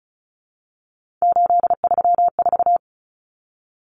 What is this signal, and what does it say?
Morse code, heard as a keyed tone, about 34 wpm, 700 Hz. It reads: 834
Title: Common Cut Numbers - 3-Digit 35wpm